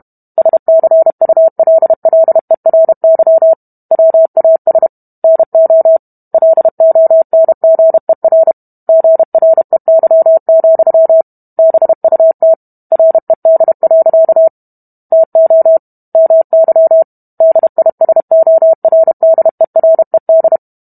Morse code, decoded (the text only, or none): SCULLERY WAS NO LONGER GREY, BUT RED. TO MY DISORDERED